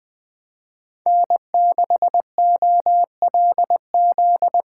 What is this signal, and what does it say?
Morse code, heard as a keyed tone, about 24 wpm, 700 Hz. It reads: N6OLZ
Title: Speed Racing - USA 1x3 Call Signs 20wpm